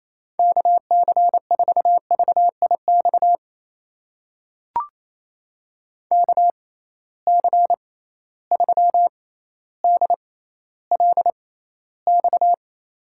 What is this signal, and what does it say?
Morse code, heard as a keyed tone, about 28 wpm, spaced as for 14 wpm, 700 Hz. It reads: KC4VIX KC3DLX
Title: USA Call Signs - 6 Characters - 6x Character Spacing - 28wpm